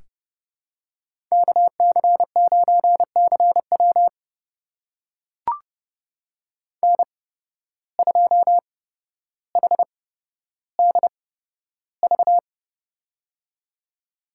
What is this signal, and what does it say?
Morse code, heard as a keyed tone, about 31 wpm, spaced as for 13 wpm, 700 Hz. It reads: KC9CW N2HDV
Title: USA Call Signs - 5 Characters - 8x Character Spacing - 30wpm